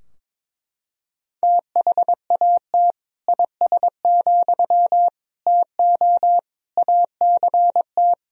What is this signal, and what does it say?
Morse code, heard as a keyed tone, about 21 wpm, 700 Hz. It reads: THAT IS, TO ACT
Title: Sentences from Top 1000 Words - Review Part 4 - 22wpm